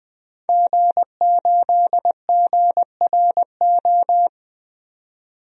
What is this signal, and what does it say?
Morse code, heard as a keyed tone, about 20 wpm, 700 Hz. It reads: G8GRO
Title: DXCC-100 Call Signs - 5 Characters - 20wpm